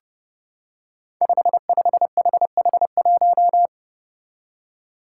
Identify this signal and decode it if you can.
Morse code, heard as a keyed tone, about 29 wpm, 700 Hz. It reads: 55HH1
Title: Five Challenging Letter-Number Combinations 30wpm